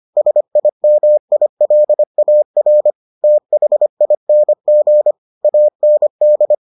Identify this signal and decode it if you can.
Morse code, heard as a keyed tone, about 25 wpm, 600 Hz. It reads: SIMILAR THING AND